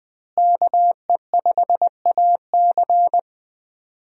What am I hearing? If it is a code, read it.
Morse code, heard as a keyed tone, about 20 wpm, 700 Hz. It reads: KE5AC
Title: USA Call Signs - 5 Characters - Repeated 2x - 20wpm